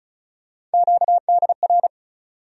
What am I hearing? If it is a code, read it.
Morse code, heard as a keyed tone, about 36 wpm, 700 Hz. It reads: QDR
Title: USA Call Sign Postfixes - 3 Letters 35wpm